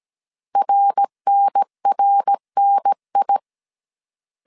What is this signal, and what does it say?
Morse code, heard as a keyed tone, about 17 wpm, 800 Hz. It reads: RNRNI